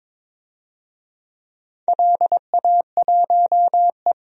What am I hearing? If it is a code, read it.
Morse code, heard as a keyed tone, about 23 wpm, 700 Hz. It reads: LA1E
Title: DXCC-100 Call Signs - 4 Characters - 22wpm